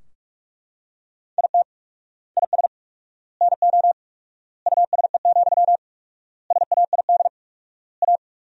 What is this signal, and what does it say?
Morse code, heard as a keyed tone, about 42 wpm, 700 Hz. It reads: IT IS NO USE, SAID A
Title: 4x Word-Spacing - Sentences from Top 500 Words - 45wpm